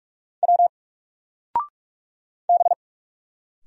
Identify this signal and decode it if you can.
Morse code, heard as a keyed tone, about 45 wpm, 700 Hz. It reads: W B
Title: Single Letter-Number 45wpm